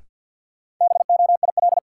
Morse code, heard as a keyed tone, about 52 wpm, 700 Hz. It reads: BOIL
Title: Sets of 5 Words Encore - New Word Drill - 50wpm